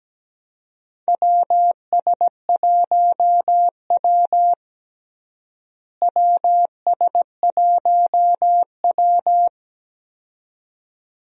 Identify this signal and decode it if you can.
Morse code, heard as a keyed tone, about 17 wpm, 700 Hz. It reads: WS1W WS1W
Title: USA Call Signs - 4 Characters - Repeated 2x - 17wpm